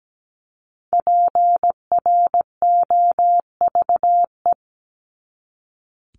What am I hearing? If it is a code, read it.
Morse code, heard as a keyed tone, about 17 wpm, 700 Hz. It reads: PROVE
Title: Top 1251-1500 Words 17wpm